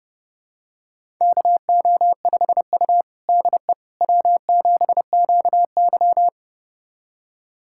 Morse code, heard as a keyed tone, about 30 wpm, 700 Hz. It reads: KO5U DE W7QY
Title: USA Callsign DE Callsign - 4 Characters - 30wpm